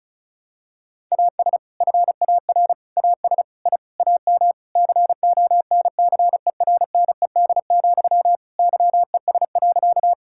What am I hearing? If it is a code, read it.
Morse code, heard as a keyed tone, about 34 wpm, 700 Hz. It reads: AS FAR AS I AM CONCERNED, YES.
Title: Sentences from Top 1000 Words - Review Part 6 - 35wpm